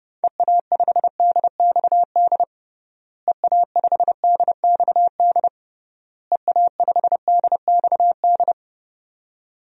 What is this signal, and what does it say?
Morse code, heard as a keyed tone, about 30 wpm, 700 Hz. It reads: EA5DXD EA5DXD EA5DXD
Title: DXCC-100 Call Signs - 6 Characters - Repeated 3x - 30wpm